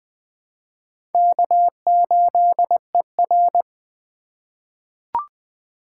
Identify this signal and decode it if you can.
Morse code, heard as a keyed tone, about 20 wpm, 700 Hz. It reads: K8ER
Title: USA Call Signs - 4 Characters - 4x Character Spacing - 20wpm